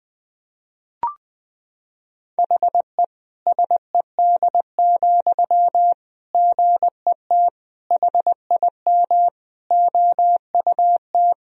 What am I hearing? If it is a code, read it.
Morse code, heard as a keyed tone, about 21 wpm, 700 Hz. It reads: HE SED, GET HIM OUT
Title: Sentences from Top 400 Words and Abbreviations 20wpm